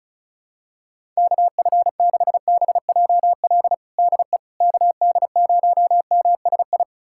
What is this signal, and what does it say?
Morse code, heard as a keyed tone, about 34 wpm, 700 Hz. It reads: KF6BJL DE KD0MSI
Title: USA Callsign DE Callsign - 6 Characters - 35wpm